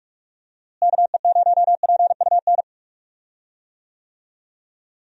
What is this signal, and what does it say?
Morse code, heard as a keyed tone, about 41 wpm, 700 Hz. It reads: KE0PUN
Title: USA Call Signs - 6 Characters - Repeated 2x - 45wpm